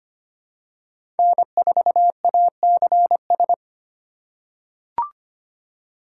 Morse code, heard as a keyed tone, about 25 wpm, 700 Hz. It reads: N4ACS
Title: USA Call Signs - 5 Characters - 2x Character Spacing - 25wpm